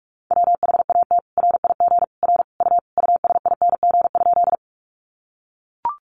Morse code, heard as a keyed tone, about 44 wpm, 700 Hz. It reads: WHAT RIG R U USING?
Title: Common QSO Phrases 45wpm